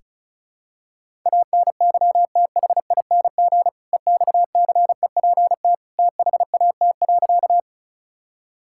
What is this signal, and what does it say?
Morse code, heard as a keyed tone, about 36 wpm, 700 Hz. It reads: ANYTHING EXCEPT THAT.
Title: Sentences from Top 1600 Words 35wpm